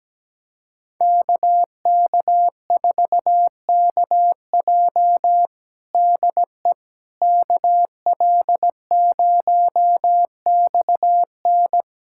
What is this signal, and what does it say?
Morse code, heard as a keyed tone, about 17 wpm, 700 Hz. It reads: KK4KJ DE KL0XN
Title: USA Callsign DE Callsign - 5 Characters - 17wpm